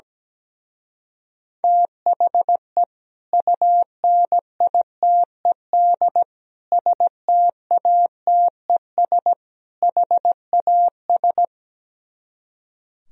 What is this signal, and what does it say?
Morse code, heard as a keyed tone, about 17 wpm, 700 Hz. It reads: THE UNITED STATES HAS
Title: Speed-Racing - Sets of 4 Words 17wpm